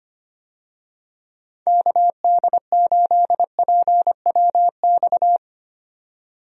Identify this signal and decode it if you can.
Morse code, heard as a keyed tone, about 25 wpm, 700 Hz. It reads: KD8PWX
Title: USA Call Signs - 6 Characters - 25wpm